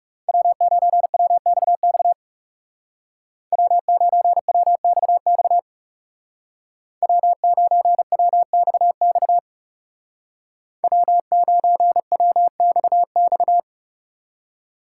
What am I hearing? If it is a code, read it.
Morse code, heard as a keyed tone, about 35 wpm, 700 Hz. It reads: W9WXX W9WXX W9WXX W9WXX
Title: Speed Racing - USA 1x3 Call Signs 30wpm